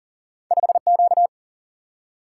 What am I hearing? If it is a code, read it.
Morse code, heard as a keyed tone, about 40 wpm, 700 Hz. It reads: 5Q
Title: Single-Letter Single-Number 40wpm